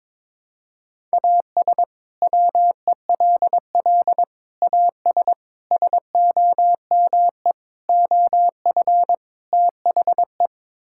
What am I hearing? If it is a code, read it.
Morse code, heard as a keyed tone, about 22 wpm, 700 Hz. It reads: AS WELL AS SOME OF THE
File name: practice / 22wpm / Sets of 6 Words - 22wpm